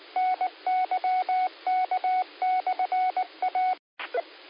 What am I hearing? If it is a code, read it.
Morse code, heard as a keyed tone, about 19 wpm, 740 Hz. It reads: NYK/A
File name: qsorec SimplexLogic